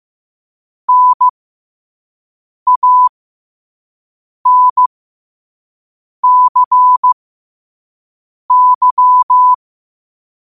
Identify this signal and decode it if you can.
Morse code, heard as a keyed tone, about 15 wpm, spaced as for 5 wpm, 1000 Hz. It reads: NANCY
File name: qotdfiles / 5wpm